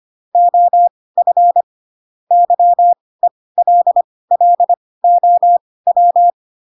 Morse code, heard as a keyed tone, about 25 wpm, spaced as for 18 wpm, 700 Hz. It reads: OF YELLOW
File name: Pinocchio / Pinocchio ch2 12 25 ik1hge